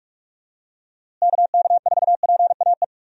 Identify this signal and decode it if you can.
Morse code, heard as a keyed tone, about 46 wpm, 700 Hz. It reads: KK4PAE
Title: USA Call Signs - 6 Characters - Repeated 3x - 45wpm